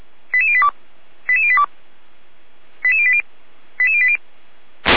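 Five-tone selective call (ZVEI), heard as quick runs of five tones, tones of about 70 ms. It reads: 88082 88082 88080 88080
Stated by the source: ZVEI five-tone selective call received by SDR.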